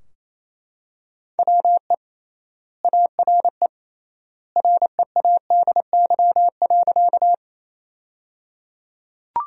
Morse code, heard as a keyed tone, about 28 wpm, 700 Hz. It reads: WE ARE READY.
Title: 3x Word-Spacing - Sentences from Top 400 Words - 28wpm